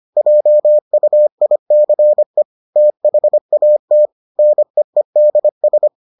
Morse code, heard as a keyed tone, about 25 wpm, 600 Hz. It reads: JUICE THAT NEEDS